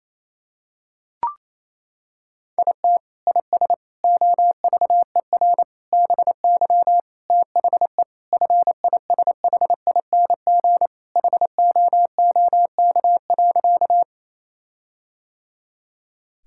Morse code, heard as a keyed tone, about 28 wpm, 700 Hz. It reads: IT IS OVER BY THE FISHING HOOK.
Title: Sentences from Top 1600 Words 28wpm